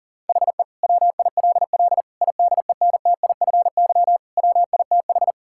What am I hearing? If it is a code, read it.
Morse code, heard as a keyed tone, about 40 wpm, 700 Hz. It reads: HE WILL IDENTIFY WITH
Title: Sentences from Top 1000 Words - Review Part 6 - 40wpm